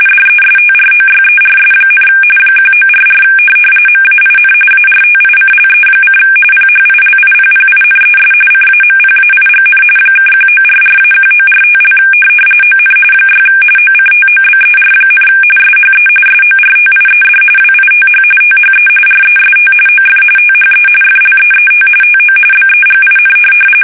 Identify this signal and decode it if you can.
RTTY, about 45.45 baud. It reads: DESTROYERS, 1944 MAIN ARTICLE: PHILIPPINES CAMPAIGN (1944-1945) ON 6 OCTOBER, SHE SORTIED WITH THE REST OF THE FLEET TO BEGIN
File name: fsk850 / USS-Massachusetts